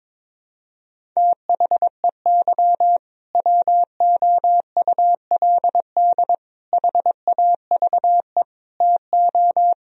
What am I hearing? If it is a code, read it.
Morse code, heard as a keyed tone, about 22 wpm, 700 Hz. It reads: THEY WOULD HAVE TO
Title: Sentences from Top 1000 Words - Review Part 1 - 22wpm